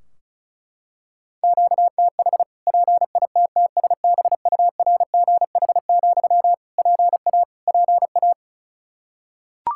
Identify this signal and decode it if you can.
Morse code, heard as a keyed tone, about 36 wpm, 700 Hz. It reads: QTH PITTSBURGH, PA PA
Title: Practice QSO - Exchange 1 - Beginning - 35wpm